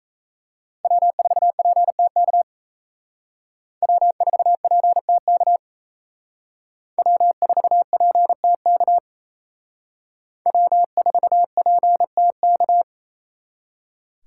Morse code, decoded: W4PTK W4PTK W4PTK W4PTK

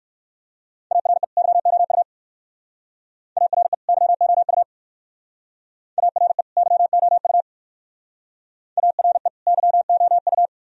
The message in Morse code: ARE YOU ARE YOU ARE YOU ARE YOU

